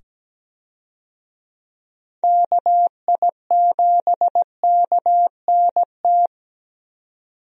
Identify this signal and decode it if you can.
Morse code, heard as a keyed tone, about 17 wpm, 700 Hz. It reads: KI7KNT
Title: USA Call Signs - 6 Characters - 17wpm